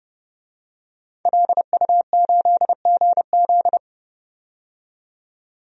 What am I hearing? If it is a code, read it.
Morse code, heard as a keyed tone, about 31 wpm, 700 Hz. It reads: LU8GZ
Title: DXCC-100 Call Signs - 5 Characters - 30wpm